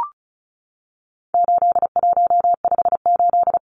Morse code, heard as a keyed tone, about 34 wpm, 700 Hz. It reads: 8158
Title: Numbers - 4-Digit 35wpm